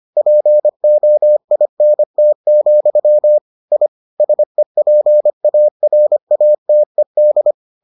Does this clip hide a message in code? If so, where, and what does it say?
Yes: Morse code from 0.2 to 7.5 s, heard as a keyed tone, about 25 wpm, 600 Hz. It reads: POINT, I SEPARATED